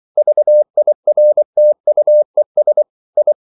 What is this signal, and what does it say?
Morse code, heard as a keyed tone, about 24 wpm, 600 Hz. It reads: VIRTUES I
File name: CW / Practice / EnglishQuotations0424